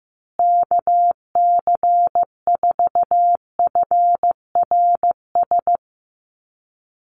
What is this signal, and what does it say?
Morse code, heard as a keyed tone, about 15 wpm, 700 Hz. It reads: KC4FRS
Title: USA Call Signs Repeated 3x 15wpm